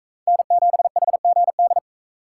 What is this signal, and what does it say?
Morse code, heard as a keyed tone, about 43 wpm, 700 Hz. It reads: N7HGD